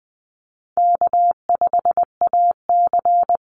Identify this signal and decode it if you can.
Morse code, heard as a keyed tone, about 19 wpm, 700 Hz. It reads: K5AC
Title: USA Call Signs 20wpm